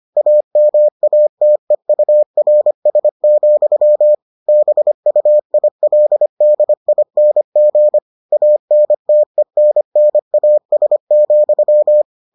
Morse code, heard as a keyed tone, about 25 wpm, 600 Hz. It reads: AMATEURS, BUILDING ANTENNAS,